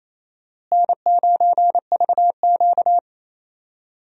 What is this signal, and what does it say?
Morse code, heard as a keyed tone, about 28 wpm, 700 Hz. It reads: N9VQ
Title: USA Call Signs - 4 Characters - Repeated 3x - 28wpm